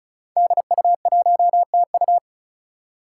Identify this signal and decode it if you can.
Morse code, heard as a keyed tone, about 36 wpm, 700 Hz. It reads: DU1TU
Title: Five Challenging Letter-Number Combinations 35wpm